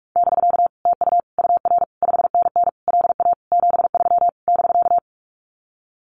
Morse code, heard as a keyed tone, about 46 wpm, 700 Hz. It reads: <BK> TU UR 5NN LA 73 <BK>
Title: Common QSO Phrases Encore 45wpm